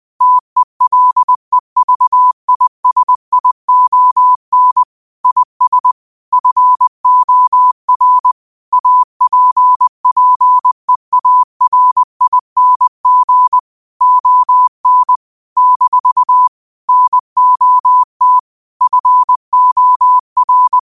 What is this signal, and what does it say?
Morse code, heard as a keyed tone, about 20 wpm, 1000 Hz. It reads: TELEVISION IS FOR APPEARING ON - NOT FOR
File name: qotdfiles / 20wpm